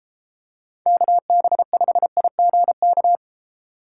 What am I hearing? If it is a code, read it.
Morse code, heard as a keyed tone, about 32 wpm, 700 Hz. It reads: KB5IGK